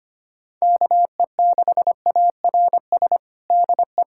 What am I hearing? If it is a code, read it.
Morse code, heard as a keyed tone, about 25 wpm, 700 Hz. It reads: KE6ARS DE
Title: USA Callsign DE Callsign - 6 Characters - 25wpm